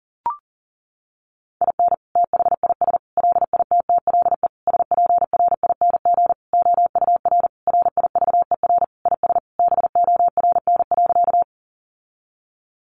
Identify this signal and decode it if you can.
Morse code, heard as a keyed tone, about 40 wpm, 700 Hz. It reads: IN THIS LITTLE SPRING OUR RIVER IS BORN.
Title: Sentences from Top 1200 Words 40wpm